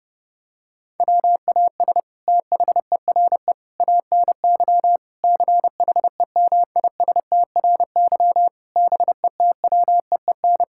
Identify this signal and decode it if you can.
Morse code, heard as a keyed tone, about 30 wpm, 700 Hz. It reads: WAS THERE ANY CHEMISTRY BETWEEN
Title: Sentences from Top 1000 Words - Review Part 6 - 30wpm